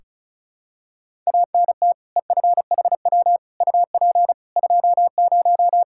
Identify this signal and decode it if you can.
Morse code, heard as a keyed tone, about 36 wpm, 700 Hz. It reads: ANT EFHW UP 20
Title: Practice QSO - Exchange 2 - Info - 35wpm